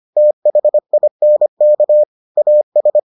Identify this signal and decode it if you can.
Morse code, heard as a keyed tone, about 25 wpm, 600 Hz. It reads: THINK AS